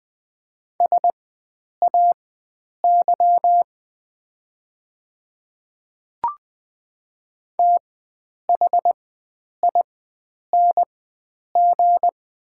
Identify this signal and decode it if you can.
Morse code, heard as a keyed tone, about 20 wpm, spaced as for 9 wpm, 700 Hz. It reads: SAY THING
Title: 4x Character Spacing - Top 300 Words - 20wpm